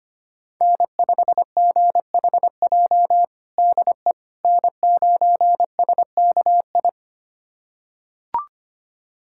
Morse code, heard as a keyed tone, about 25 wpm, 700 Hz. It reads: N5GHJ DE N9SKI
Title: USA Callsign DE Callsign - 5 Characters - 25wpm